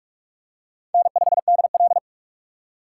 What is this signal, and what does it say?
Morse code, heard as a keyed tone, about 42 wpm, 700 Hz. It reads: N5DL
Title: USA Call Signs - 4 Characters - 4x Character Spacing - 45wpm